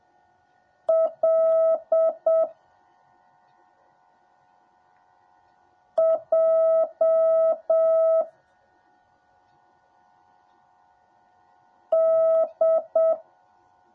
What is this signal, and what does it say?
Morse code, heard as a keyed tone, about 7 wpm, 650 Hz. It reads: L J D
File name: jh8cbh / 2021cw / cw10a